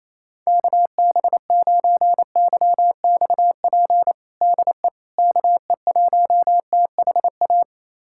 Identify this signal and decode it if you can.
Morse code, heard as a keyed tone, about 28 wpm, 700 Hz. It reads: KB9YXP DE KE1THA
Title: USA Callsign DE Callsign 28wpm